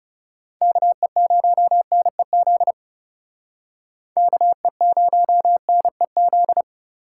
Morse code, heard as a keyed tone, about 37 wpm, 700 Hz. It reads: KE0NEZ KE0NEZ
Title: Speed Racing - USA 2x3 Call Signs 30wpm